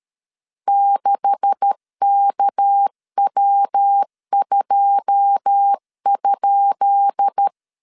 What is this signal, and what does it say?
Morse code, heard as a keyed tone, about 13 wpm, 800 Hz. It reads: 6KW2?
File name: LEZIONI / 10 SIGNS / 10B INT 12wpm 2414